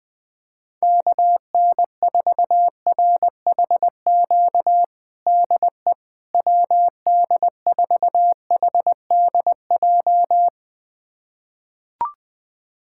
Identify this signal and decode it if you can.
Morse code, heard as a keyed tone, about 21 wpm, 700 Hz. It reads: KN4RHQ DE WD4HDJ
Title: USA Callsign DE Callsign - 6 Characters - 20wpm